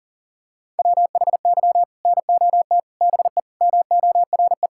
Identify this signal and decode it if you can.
Morse code, heard as a keyed tone, about 40 wpm, 700 Hz. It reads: WHY NOT BE MORE
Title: Sentences from Top 1600 Words 40wpm